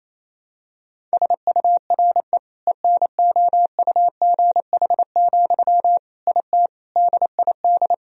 Morse code, heard as a keyed tone, about 28 wpm, 700 Hz. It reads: SURE ENOUGH, IT DID
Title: Sentences from Top 1000 Words - Review Part 2 - 28wpm